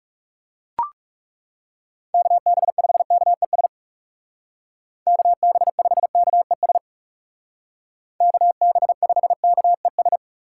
Morse code, heard as a keyed tone, about 36 wpm, 700 Hz. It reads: KB5KES KB5KES KB5KES
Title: Speed Racing - USA 2x3 Call Signs 30wpm